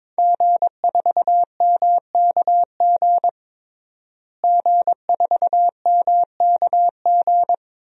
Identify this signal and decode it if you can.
Morse code, heard as a keyed tone, about 22 wpm, 700 Hz. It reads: G4MKG G4MKG
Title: DXCC-100 Call Signs - 5 Characters - Repeated 3x - 22wpm